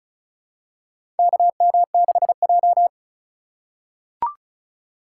Morse code, heard as a keyed tone, about 36 wpm, 700 Hz. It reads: KM6J
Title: USA Call Signs - 4x Character Spacing - 35wpm